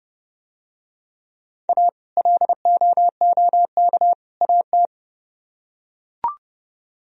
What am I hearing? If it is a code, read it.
Morse code, heard as a keyed tone, about 28 wpm, 700 Hz. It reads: A LOOK AT